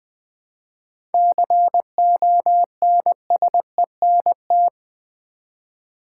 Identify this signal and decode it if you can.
Morse code, heard as a keyed tone, about 20 wpm, 700 Hz. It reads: CONSENT
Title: 7 Letter Words - First 3 Letters Given - 20wpm